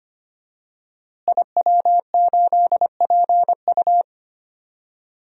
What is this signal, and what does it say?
Morse code, heard as a keyed tone, about 25 wpm, 700 Hz. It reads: IW8PU
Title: DXCC-100 Call Signs - 5 Characters - Repeated 2x - 25wpm